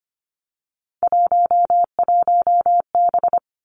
Morse code, heard as a keyed tone, about 25 wpm, 700 Hz. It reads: 11B
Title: Three Challenging Letter-Number Combinations 25wpm